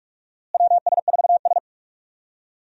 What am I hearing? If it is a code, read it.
Morse code, heard as a keyed tone, about 44 wpm, 700 Hz. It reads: WS4S
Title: USA Call Signs - 4 Characters - Repeated 2x - 45wpm